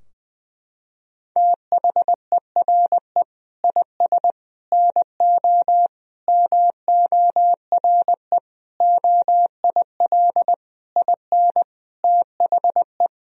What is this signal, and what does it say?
Morse code, heard as a keyed tone, about 20 wpm, 700 Hz. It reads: THERE IS NO MORE OIL IN THE
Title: Sentences from Top 1000 Words - Review Part 4 - 20wpm